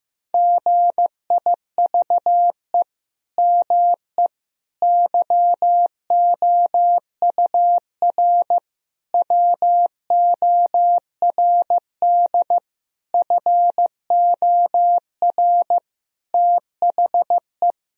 Morse code, heard as a keyed tone, about 16 wpm, 700 Hz. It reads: GIVE ME YOUR WORD FOR THE
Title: Sentences from Top 1000 Words - Review Part 3 - 15wpm